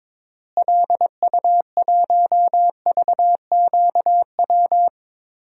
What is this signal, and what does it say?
Morse code, heard as a keyed tone, about 22 wpm, 700 Hz. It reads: LU1VQW
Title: DXCC-100 Call Signs - 6 Characters - Repeated 2x - 22wpm